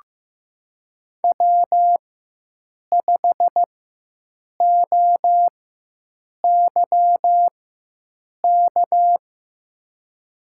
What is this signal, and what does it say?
Morse code, heard as a keyed tone, about 15 wpm, spaced as for 7 wpm, 700 Hz. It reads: W5OYK
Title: USA Call Signs - 4x Character Spacing - 15wpm